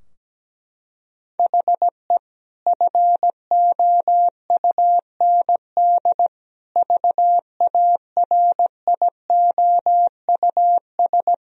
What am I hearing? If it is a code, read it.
Morse code, heard as a keyed tone, about 17 wpm, 700 Hz. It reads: HE FOUND VARIOUS
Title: Sentences from Top 1000 Words - Review Part 4 - 17wpm